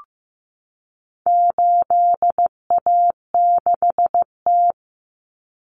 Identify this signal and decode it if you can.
Morse code, heard as a keyed tone, about 15 wpm, 700 Hz. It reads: 8A6T
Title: Common Cut Numbers - 4-Digit 15wpm